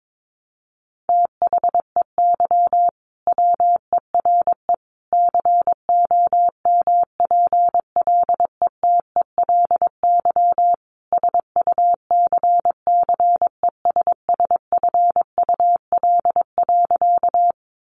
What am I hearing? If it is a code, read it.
Morse code, heard as a keyed tone, about 22 wpm, 700 Hz. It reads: THEY WERE COMPLETELY SUCCESSFUL.
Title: Sentences from Top 1100 Words 22wpm